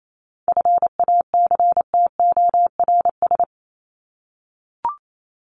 Morse code, heard as a keyed tone, about 28 wpm, 700 Hz. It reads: FACTORS
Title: ES:S Suffix - Top 100 Words with Base Words 28wpm